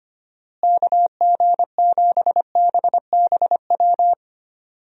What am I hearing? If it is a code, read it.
Morse code, heard as a keyed tone, about 25 wpm, 700 Hz. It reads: KG7BBW
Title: USA Call Signs - 6 Characters - Repeated 3x - 25wpm